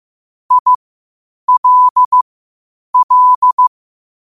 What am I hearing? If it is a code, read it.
Morse code, heard as a keyed tone, about 15 wpm, spaced as for 8 wpm, 1000 Hz. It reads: ILL
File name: qotdfiles / 8wpm